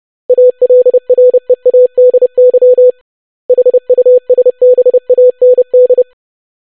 Morse code, heard as a keyed tone, about 30 wpm, 500 Hz. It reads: ALREADY HUSBAND
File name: basic500-25wpm